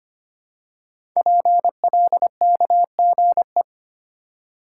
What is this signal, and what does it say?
Morse code, heard as a keyed tone, about 25 wpm, 700 Hz. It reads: PLKGE
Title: Code Groups - 5 Letters - 25wpm